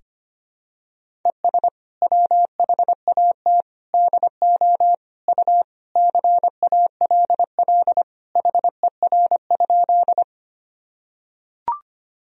Morse code, heard as a keyed tone, about 25 wpm, 700 Hz. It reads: ES WHAT DO U CALL HER?
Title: Sentences from Top 100 Words and Abbreviations 25wpm